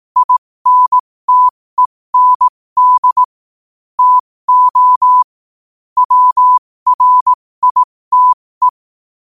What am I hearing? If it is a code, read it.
Morse code, heard as a keyed tone, about 18 wpm, spaced as for 15 wpm, 1000 Hz. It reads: INTEND TO WRITE
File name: qotdfiles / 15wpm